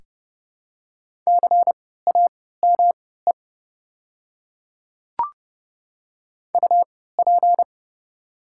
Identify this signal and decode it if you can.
Morse code, heard as a keyed tone, about 30 wpm, spaced as for 17 wpm, 700 Hz. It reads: CAME UP
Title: 3x Character Spacing - Top 500 Words - 30wpm